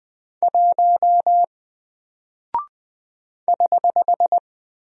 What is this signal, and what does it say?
Morse code, heard as a keyed tone, about 20 wpm, 700 Hz. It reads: 1 <HH>
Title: Single Character (Letter:Number:Punctuation:Prosign) Alt 20wpm